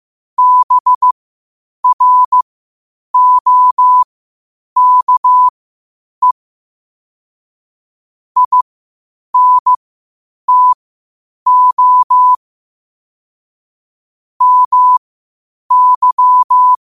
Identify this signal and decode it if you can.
Morse code, heard as a keyed tone, about 15 wpm, spaced as for 8 wpm, 1000 Hz. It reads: BROKE INTO MY
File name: qotdfiles / 8wpm